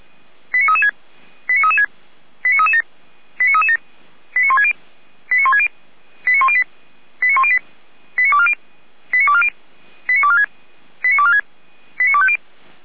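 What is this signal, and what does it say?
Five-tone selective call (ZVEI), heard as quick runs of five tones, tones of about 70 ms. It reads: 89336 89336 89337 89337 89160 89160 89108 89108 89240 89240 89246 89246 89250
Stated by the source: ZVEI five-tone call received on SDR.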